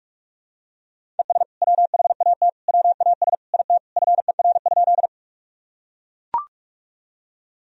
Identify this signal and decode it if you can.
Morse code, heard as a keyed tone, about 41 wpm, 700 Hz. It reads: ES WHAT WAS IT FER?
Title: Sentences from Top 100 Words and Abbreviations 45wpm